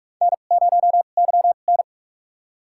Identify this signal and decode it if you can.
Morse code, heard as a keyed tone, about 46 wpm, spaced as for 32 wpm, 700 Hz. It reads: N0YN
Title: USA Call Signs - 4 Characters - 2x Character Spacing - 45wpm